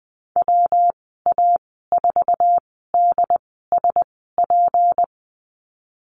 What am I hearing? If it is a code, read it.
Morse code, heard as a keyed tone, about 19 wpm, spaced as for 14 wpm, 700 Hz. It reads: WA4DSP
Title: 154 Lesson 25g - Letter p - Call Signs 20wpm